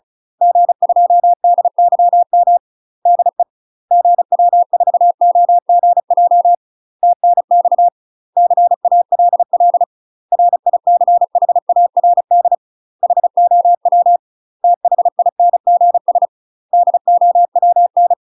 Morse code, heard as a keyed tone, about 34 wpm, 700 Hz. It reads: G2DYM DE GW4OGJ TNX CALL RICHARD HOW THINGS DOWN
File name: UK MASTER QSOs TRAINING / 35WPM / UK MASTER QSOs 0018